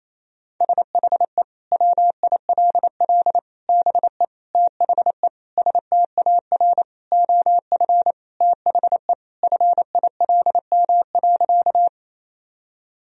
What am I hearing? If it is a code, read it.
Morse code, heard as a keyed tone, about 28 wpm, 700 Hz. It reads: SHE WILL BE THE STAR OF THE FILM.
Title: Sentences from Top 1000 Words - Review Part 3 - 28wpm